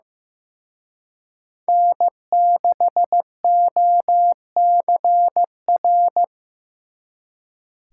Morse code, heard as a keyed tone, about 18 wpm, 700 Hz. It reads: N6OCR
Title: Speed Racing - USA 1x3 Call Signs 15wpm